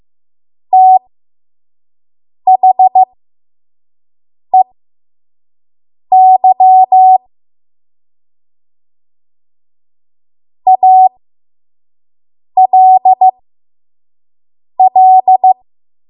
Morse code, heard as a keyed tone, about 15 wpm, spaced as for 5 wpm, 750 Hz. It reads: THEY ALL
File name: Morse / 05WPM